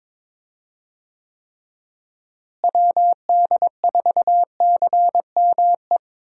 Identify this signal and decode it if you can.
Morse code, heard as a keyed tone, about 21 wpm, 700 Hz. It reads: WD4CME
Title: USA Call Signs - 6 Characters - Repeated 3x - 22wpm